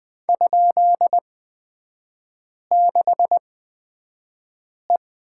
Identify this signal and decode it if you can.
Morse code, heard as a keyed tone, about 20 wpm, spaced as for 5 wpm, 700 Hz. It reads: ?6E
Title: 236 Lesson 38e - Letter q - Three Characters 20wpm